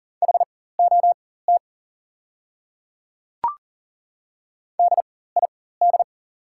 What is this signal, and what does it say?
Morse code, heard as a keyed tone, about 40 wpm, spaced as for 19 wpm, 700 Hz. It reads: HOT DID
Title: 4x Character Spacing - Top 100 Words - 40wpm